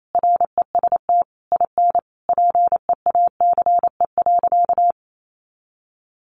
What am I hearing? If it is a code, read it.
Morse code, heard as a keyed tone, about 28 wpm, 700 Hz. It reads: REST IN PEACE.
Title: Sentences from Top 1500 Words 28wpm